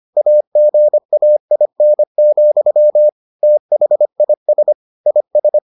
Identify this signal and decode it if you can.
Morse code, heard as a keyed tone, about 25 wpm, 600 Hz. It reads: AGAIN, THIS IS